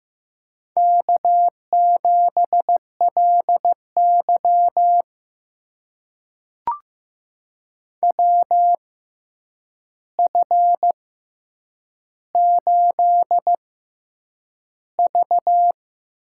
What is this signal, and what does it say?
Morse code, heard as a keyed tone, about 15 wpm, spaced as for 8 wpm, 700 Hz. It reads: K7LY WF8V
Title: USA Call Signs - 4 Characters - 6x Character Spacing - 15wpm